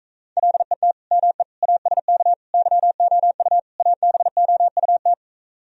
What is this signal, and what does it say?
Morse code, heard as a keyed tone, about 45 wpm, 700 Hz. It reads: LET ME ASK YOU ABOUT